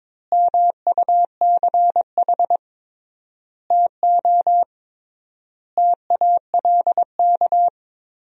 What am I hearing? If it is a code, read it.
Morse code, heard as a keyed tone, about 21 wpm, 700 Hz. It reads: MUCH TO TALK
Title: 3x Word-Spacing - Sentences from Top 400 Words - 22wpm